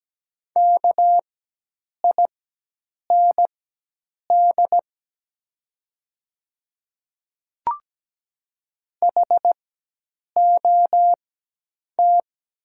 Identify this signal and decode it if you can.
Morse code, heard as a keyed tone, about 17 wpm, spaced as for 8 wpm, 700 Hz. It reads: KIND HOT
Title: 4x Character Spacing - Top 300 Words - 17wpm